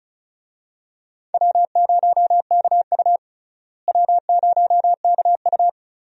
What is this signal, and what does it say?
Morse code, heard as a keyed tone, about 36 wpm, 700 Hz. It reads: W0KU W0KU
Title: USA Call Signs - 4 Characters - Repeated 2x - 35wpm